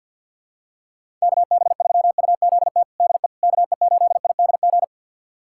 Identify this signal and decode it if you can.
Morse code, heard as a keyed tone, about 52 wpm, 700 Hz. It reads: KB3UZT DE KE8IDG
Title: USA Callsign DE Callsign 50wpm